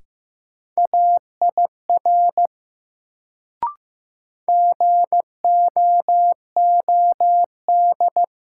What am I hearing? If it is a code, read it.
Morse code, heard as a keyed tone, about 15 wpm, 700 Hz. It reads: AIR GOOD
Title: Sentences from Top 200 Words - New Word Drill - 15wpm